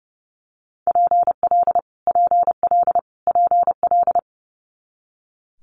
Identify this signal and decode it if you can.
Morse code, heard as a keyed tone, about 28 wpm, 700 Hz. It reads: PL PL PL
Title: Multi-letter Phonemes Repeated 3x 30wpm